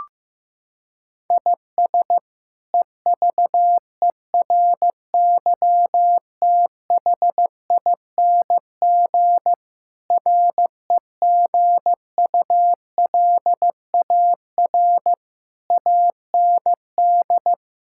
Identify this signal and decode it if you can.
Morse code, heard as a keyed tone, about 15 wpm, 700 Hz. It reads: IS EVERYTHING REGULAR AND
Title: Sentences from Top 1000 Words - Review Part 6 - 15wpm